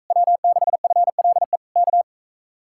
Morse code, heard as a keyed tone, about 45 wpm, 700 Hz. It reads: W6FLE K